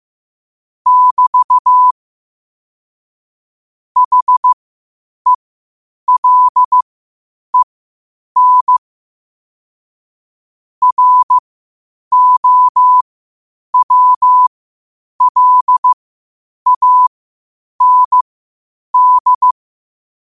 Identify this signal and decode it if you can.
Morse code, heard as a keyed tone, about 15 wpm, spaced as for 8 wpm, 1000 Hz. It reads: <BT> HELEN ROWLAND
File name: 8wpm